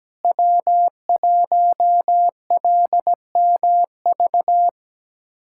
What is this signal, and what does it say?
Morse code, heard as a keyed tone, about 17 wpm, 700 Hz. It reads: W1LMV
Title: USA Callsign DE Callsign - 5 Characters - 17wpm